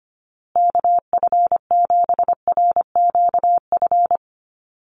Morse code, heard as a keyed tone, about 25 wpm, 700 Hz. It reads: KF7RQF
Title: USA Call Signs Repeated 2x 25wpm